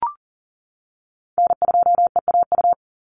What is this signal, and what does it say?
Morse code, heard as a keyed tone, about 40 wpm, 700 Hz. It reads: N2EAU
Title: USA Call Signs 40wpm